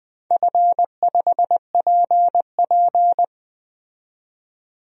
Morse code, heard as a keyed tone, about 20 wpm, 700 Hz. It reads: F5PP
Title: DXCC-100 Call Signs - 4 Characters - 20wpm